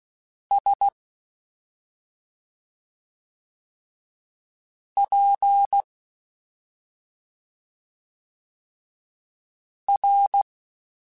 Morse code, heard as a keyed tone, about 16 wpm, spaced as for 2 wpm, 800 Hz. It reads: SPR
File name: Special INGs-mp3s / Special INGs-mp3s / Slow ing @10